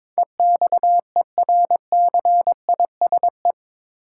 Morse code, heard as a keyed tone, about 22 wpm, 700 Hz. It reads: EXERCISE
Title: Sentences from Top 1200 Words - New Word Drill Alt - 22wpm